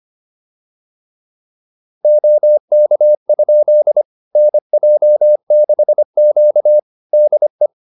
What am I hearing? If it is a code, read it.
Morse code, heard as a keyed tone, about 25 wpm, 600 Hz. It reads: OK? NJ6Q DE